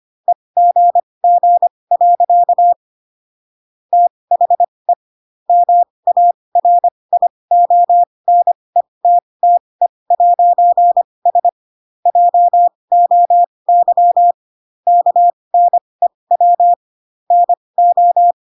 Morse code, heard as a keyed tone, about 25 wpm, spaced as for 20 wpm, 700 Hz. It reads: EGG. THE MARIONETTE'S JOY KNEW NO